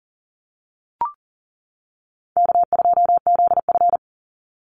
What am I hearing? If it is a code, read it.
Morse code, heard as a keyed tone, about 40 wpm, 700 Hz. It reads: K2ZF
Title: Easy Call Signs - 40wpm